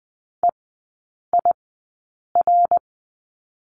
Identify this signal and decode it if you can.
Morse code, heard as a keyed tone, about 20 wpm, spaced as for 8 wpm, 700 Hz. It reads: EIR
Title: 074 Lesson 13e - Letter L - Three Characters 20wpm